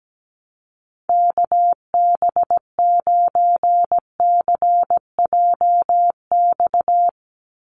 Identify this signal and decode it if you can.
Morse code, heard as a keyed tone, about 17 wpm, 700 Hz. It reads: KB9CJX
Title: USA Call Signs 17wpm